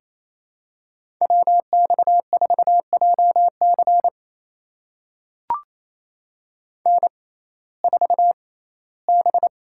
Morse code, heard as a keyed tone, about 28 wpm, spaced as for 17 wpm, 700 Hz. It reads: WX4JC N4B
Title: USA Call Signs - 5 Characters - 6x Character Spacing - 28wpm